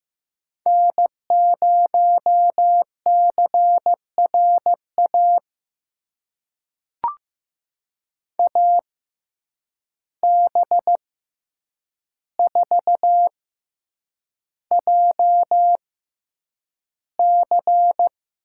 Morse code, heard as a keyed tone, about 15 wpm, spaced as for 8 wpm, 700 Hz. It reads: N0CRA AB4JC
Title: USA Call Signs - 5 Characters - 6x Character Spacing - 15wpm